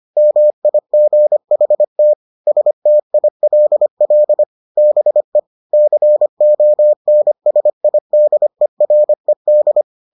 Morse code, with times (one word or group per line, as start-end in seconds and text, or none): 0.2-2.1 MIGHT
2.5-4.4 STILL
4.8-5.4 BE
5.7-9.8 CONSIDERED